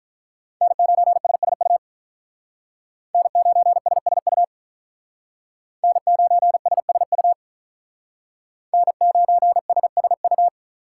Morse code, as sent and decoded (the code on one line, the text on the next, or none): -. ----. ... ... ..- / -. ----. ... ... ..- / -. ----. ... ... ..- / -. ----. ... ... ..-
N9SSU N9SSU N9SSU N9SSU